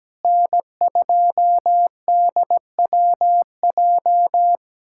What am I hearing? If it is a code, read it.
Morse code, heard as a keyed tone, about 17 wpm, 700 Hz. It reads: N2DWJ
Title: USA Callsign DE Callsign - 5 Characters - 17wpm